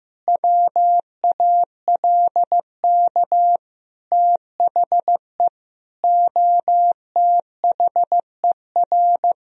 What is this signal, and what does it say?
Morse code, heard as a keyed tone, about 15 wpm, 700 Hz. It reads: WALK THE OTHER
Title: Sentences from Top 1000 Words - Review Part 1 - 15wpm